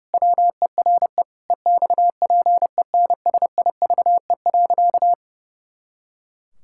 Morse code, heard as a keyed tone, about 30 wpm, 700 Hz. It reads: WERE EXPENSIVE.
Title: Pangram Challenge v2 - 30wpm